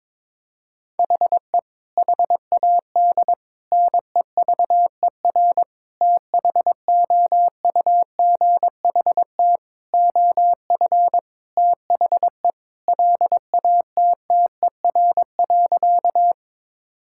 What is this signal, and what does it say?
Morse code, heard as a keyed tone, about 22 wpm, 700 Hz. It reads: HE HAD NEVER THOUGHT OF THE LATTER.
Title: Sentences from Top 1600 Words 22wpm